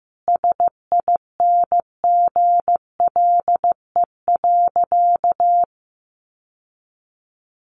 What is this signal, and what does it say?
Morse code, heard as a keyed tone, about 15 wpm, 700 Hz. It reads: SINGLE.
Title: Sentences from Top 700 Words 15wpm